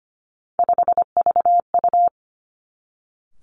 Morse code, heard as a keyed tone, about 25 wpm, 700 Hz. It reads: 5VU
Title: Three Challenging Letter-Number Combinations 25wpm